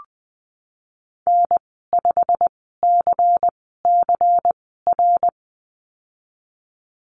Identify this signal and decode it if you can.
Morse code, heard as a keyed tone, about 20 wpm, spaced as for 15 wpm, 700 Hz. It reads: N5CCR
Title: 186 Lesson 30g - Letter b - Call Signs 20wpm